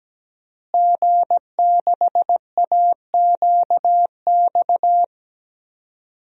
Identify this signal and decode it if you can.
Morse code, heard as a keyed tone, about 17 wpm, 700 Hz. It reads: G6AQX
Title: DXCC-100 Call Signs - 5 Characters - Repeated 3x - 17wpm